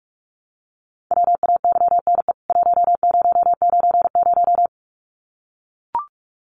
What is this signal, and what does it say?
Morse code, heard as a keyed tone, about 44 wpm, 700 Hz. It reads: WAYNE 1090
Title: CWT Exchange 45wpm